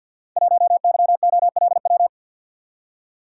Morse code, heard as a keyed tone, about 49 wpm, 700 Hz. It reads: 1YOLW
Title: Code Groups - 5 Letter:Numbers - 50wpm